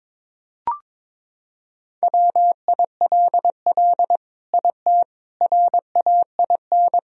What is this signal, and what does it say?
Morse code, heard as a keyed tone, about 22 wpm, 700 Hz. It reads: WILL IT RAIN
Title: Sentences from Top 1000 Words - Review Part 1 - 22wpm